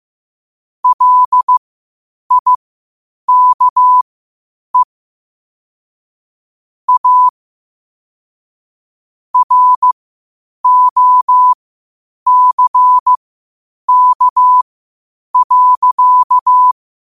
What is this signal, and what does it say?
Morse code, heard as a keyed tone, about 15 wpm, spaced as for 8 wpm, 1000 Hz. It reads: LIKE A ROCK.
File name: qotdfiles / 8wpm